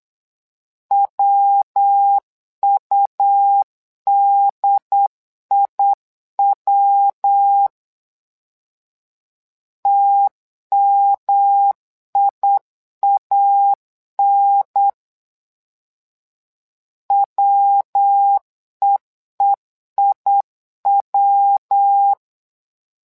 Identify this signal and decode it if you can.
Morse code, heard as a keyed tone, about 8 wpm, 800 Hz. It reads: WUDIW TMIAN WEEIW
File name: LEZIONI / 3 MWRO / 3 MWRO 8wpm 2606